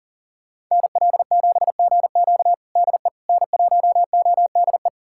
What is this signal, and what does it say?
Morse code, heard as a keyed tone, about 40 wpm, 700 Hz. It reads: NL7GQ DE N1ODE
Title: USA Callsign DE Callsign - 5 Characters - 40wpm